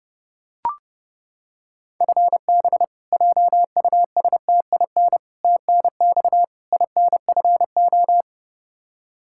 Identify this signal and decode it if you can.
Morse code, heard as a keyed tone, about 30 wpm, 700 Hz. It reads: FB JUSTIN TNX INFO
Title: Practice QSO - Exchange 3 - More Info - 30wpm